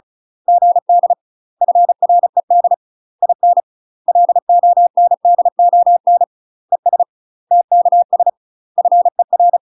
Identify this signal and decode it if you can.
Morse code, heard as a keyed tone, about 37 wpm, 700 Hz. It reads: GD FRED IN LONDON ES TKS FER
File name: UK MASTER QSOs TRAINING / 35WPM / UK MASTER QSOs 0004